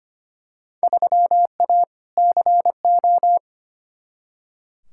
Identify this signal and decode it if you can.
Morse code, heard as a keyed tone, about 25 wpm, 700 Hz. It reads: 3A CO
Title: Field Day Exchanges Encore - 25wpm